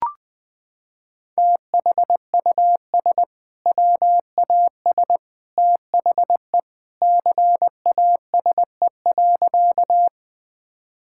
Morse code, heard as a keyed tone, about 20 wpm, 700 Hz. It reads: THUS WAS THE CASE.
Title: Sentences from Top 1000 Words - Review Part 5 - 20wpm